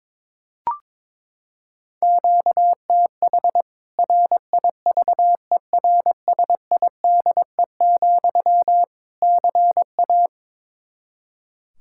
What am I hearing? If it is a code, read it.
Morse code, heard as a keyed tone, about 22 wpm, 700 Hz. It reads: QTH RIVERSIDE, CA
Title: K1USN SST QSO Practice 22wpm